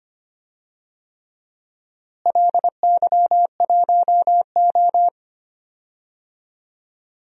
Morse code, heard as a keyed tone, about 25 wpm, 700 Hz. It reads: LY1O
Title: DXCC-100 Call Signs - 4 Characters - Repeated 3x - 25wpm